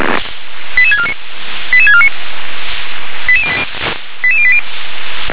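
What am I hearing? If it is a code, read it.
Five-tone selective call (ZVEI), heard as quick runs of five tones, tones of about 70 ms. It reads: 88649 88649 88080
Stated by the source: Five-tone ZVEI selective call received on an SDR.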